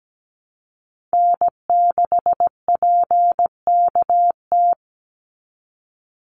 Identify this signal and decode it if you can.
Morse code, heard as a keyed tone, about 17 wpm, 700 Hz. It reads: N6PKT
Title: NA SOTA Chaser Callsigns 17wpm